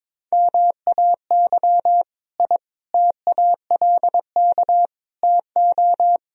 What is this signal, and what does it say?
Morse code, heard as a keyed tone, about 22 wpm, 700 Hz. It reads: MAY I TALK TO
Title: Sentences from Top 1000 Words - Review Part 4 - 22wpm